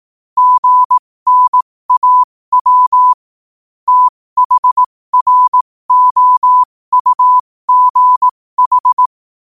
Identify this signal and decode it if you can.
Morse code, heard as a keyed tone, about 18 wpm, spaced as for 15 wpm, 1000 Hz. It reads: GNAW THROUGH
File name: qotdfiles / 15wpm